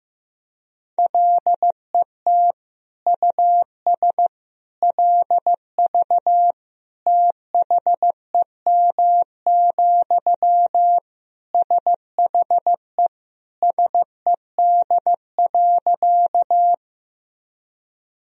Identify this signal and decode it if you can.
Morse code, heard as a keyed tone, about 15 wpm, 700 Hz. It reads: LET US LV THEM, SHE SED.
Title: Sentences from Top 400 Words and Abbreviations 15wpm